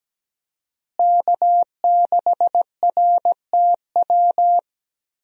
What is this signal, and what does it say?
Morse code, heard as a keyed tone, about 17 wpm, 700 Hz. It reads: K6RTW
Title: USA Call Signs - 5 Characters - Repeated 2x - 17wpm